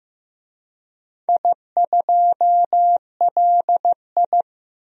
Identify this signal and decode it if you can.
Morse code, heard as a keyed tone, about 15 wpm, 700 Hz. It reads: I2LI
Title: DXCC-100 Call Signs - 4 Characters - 15wpm